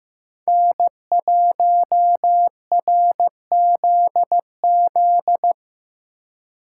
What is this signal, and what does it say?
Morse code, heard as a keyed tone, about 15 wpm, 700 Hz. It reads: N1RZZ
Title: Speed Racing - USA 1x3 Call Signs 15wpm